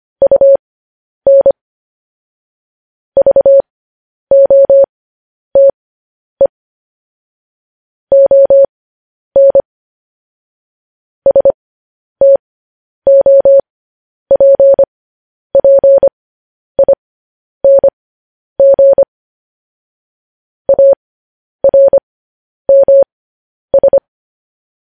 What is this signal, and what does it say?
Morse code, heard as a keyed tone, about 25 wpm, spaced as for 10 wpm, 550 Hz. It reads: UN VOTE ON STOPPING ARMS